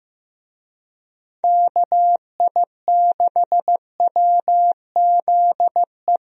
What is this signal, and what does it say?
Morse code, heard as a keyed tone, about 15 wpm, 700 Hz. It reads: KI6WZE
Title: USA Call Signs - 6 Characters - Repeated 2x - 15wpm